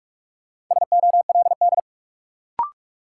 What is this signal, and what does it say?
Morse code, heard as a keyed tone, about 42 wpm, 700 Hz. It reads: SOLD
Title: Sentences from Top 1400 Words - New Word Drill Alt - 45wpm